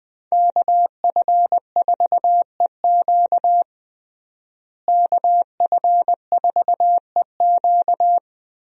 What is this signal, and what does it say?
Morse code, heard as a keyed tone, about 20 wpm, 700 Hz. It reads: KF4EQ KF4EQ
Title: USA Call Signs - 5 Characters - Repeated 2x - 20wpm